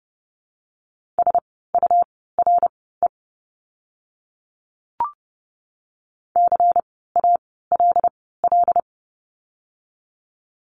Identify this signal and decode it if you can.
Morse code, heard as a keyed tone, about 30 wpm, spaced as for 17 wpm, 700 Hz. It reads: SURE CALL
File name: Lesson 17f - Letter u - Words 30wpm